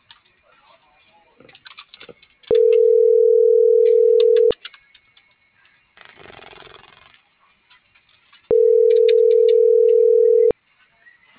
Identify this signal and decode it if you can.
Ringback tone (North America), 440 + 480 Hz.